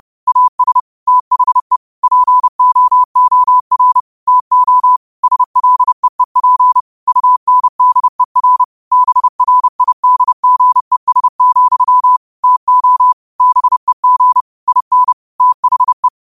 Morse code, heard as a keyed tone, about 30 wpm, 1000 Hz. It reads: AS THE POOR TO SLEEP UNDER BRIDGES, TO BEG IN THE
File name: qotdfiles / 30wpm